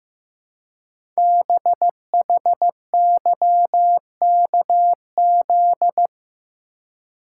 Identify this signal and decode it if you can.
Morse code, heard as a keyed tone, about 15 wpm, 700 Hz. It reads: BHYKZ
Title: Code Groups - 5 Letters - 15wpm